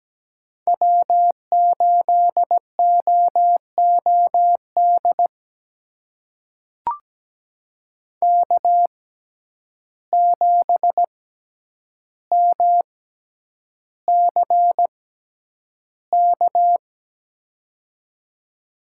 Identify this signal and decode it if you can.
Morse code, heard as a keyed tone, about 17 wpm, spaced as for 9 wpm, 700 Hz. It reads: W8OOD K7MCK
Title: USA Call Signs - 5 Characters - 6x Character Spacing - 17wpm